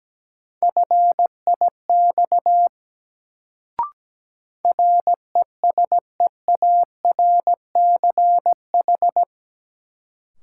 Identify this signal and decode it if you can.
Morse code, heard as a keyed tone, about 17 wpm, 700 Hz. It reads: FIX RESEARCH
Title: Sentences from Top 1400 Words - New Word Drill Alt - 17wpm